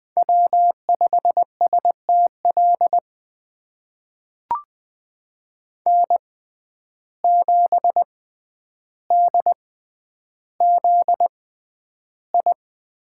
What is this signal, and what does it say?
Morse code, heard as a keyed tone, about 20 wpm, spaced as for 10 wpm, 700 Hz. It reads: W5STL N7DZI
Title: USA Call Signs - 5 Characters - 6x Character Spacing - 20wpm